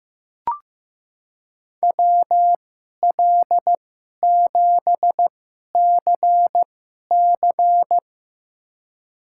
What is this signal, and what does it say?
Morse code, heard as a keyed tone, about 15 wpm, spaced as for 11 wpm, 700 Hz. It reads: WL7CC
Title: USA Call Signs - 5 Characters - 2x Character Spacing - 15wpm